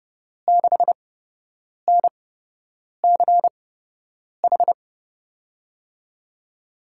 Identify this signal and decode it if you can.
Morse code, heard as a keyed tone, about 30 wpm, spaced as for 8 wpm, 700 Hz. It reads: 6NCH
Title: USA Call Signs - 5 Characters - 8x Character Spacing - 30wpm